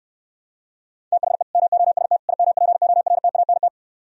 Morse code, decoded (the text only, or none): THE MOST IMPORTANT